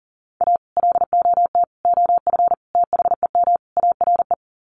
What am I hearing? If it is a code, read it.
Morse code, heard as a keyed tone, about 40 wpm, 700 Hz. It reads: A LOT OF THEM ARE
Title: Sets of 5 Words Encore2 40wpm